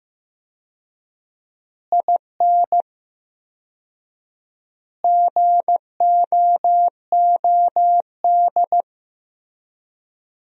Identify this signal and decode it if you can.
Morse code, heard as a keyed tone, about 15 wpm, 700 Hz. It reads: IN GOOD
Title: 4x Word-Spacing - Sentences from Top 200 Words - 15wpm